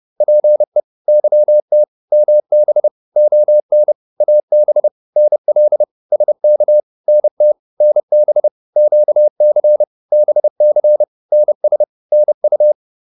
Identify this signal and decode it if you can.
Morse code, heard as a keyed tone, about 30 wpm, 600 Hz. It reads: PE YT MB ON AB NL SK NT NB QC BC NS NU
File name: Provinces-30